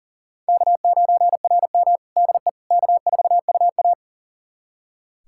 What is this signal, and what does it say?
Morse code, heard as a keyed tone, about 43 wpm, 690 Hz. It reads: K9RM DE K4UA
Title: USA Callsign DE Callsign - 4 Characters - 40wpm